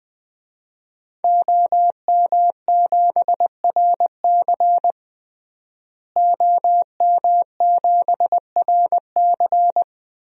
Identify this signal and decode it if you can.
Morse code, heard as a keyed tone, about 20 wpm, 700 Hz. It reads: OM7RC OM7RC
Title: DXCC-100 Call Signs - 5 Characters - Repeated 2x - 20wpm